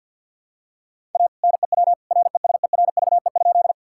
Morse code, heard as a keyed tone, about 50 wpm, 700 Hz. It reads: A NEW RESERVE?
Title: Sentences from Top 1000 Words - Review Part 6 - 50wpm